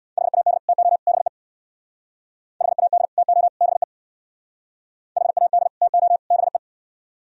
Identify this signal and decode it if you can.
Morse code, heard as a keyed tone, about 64 wpm, 700 Hz. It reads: HAD TO BE HAD TO BE HAD TO BE